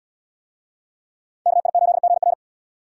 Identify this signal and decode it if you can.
Morse code, heard as a keyed tone, about 71 wpm, 700 Hz. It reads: KI8GR